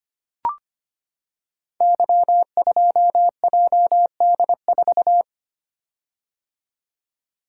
Morse code, heard as a keyed tone, about 25 wpm, 700 Hz. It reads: Y2JD4
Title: Code Groups - 5 Letter:Numbers - 25wpm